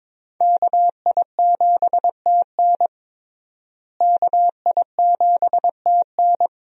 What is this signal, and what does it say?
Morse code, heard as a keyed tone, about 22 wpm, 700 Hz. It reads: KI7TN KI7TN
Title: USA Call Signs - 5 Characters - Repeated 2x - 22wpm